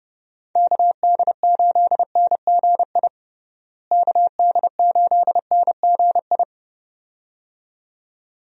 Morse code, heard as a keyed tone, about 30 wpm, 700 Hz. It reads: KD8NGI KD8NGI
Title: USA Call Signs - 6 Characters - Repeated 2x - 30wpm